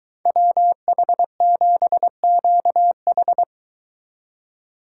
Morse code, heard as a keyed tone, about 23 wpm, 700 Hz. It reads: WH7QH